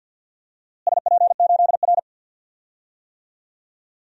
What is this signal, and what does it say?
Morse code, heard as a keyed tone, about 51 wpm, 700 Hz. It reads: SP8R
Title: DXCC-100 Call Signs - 4 Characters - Repeated 3x - 50wpm